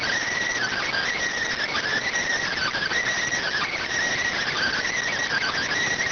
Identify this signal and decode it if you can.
Emergency Alert System (SAME) tones (header bursts), heard as harsh buzzing bursts. It is heard from 0.5 to 4.9 s.